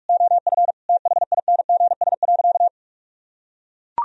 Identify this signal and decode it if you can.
Morse code, heard as a keyed tone, about 41 wpm, 700 Hz. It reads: OF THINGS.
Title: Sentences from Top 1000 Words - Review Part 4 - 45wpm